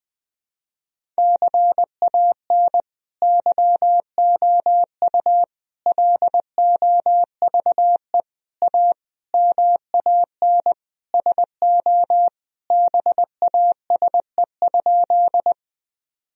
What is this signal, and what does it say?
Morse code, heard as a keyed tone, about 19 wpm, 700 Hz. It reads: CAN YOU LOVE A MAN SO BASE?
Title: Sentences from Top 1000 Words - Review Part 1 - 20wpm